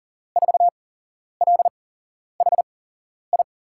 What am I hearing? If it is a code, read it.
Morse code, heard as a keyed tone, about 39 wpm, spaced as for 11 wpm, 700 Hz. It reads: 4LHI
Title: USA Call Signs - 8x Character Spacing - 40wpm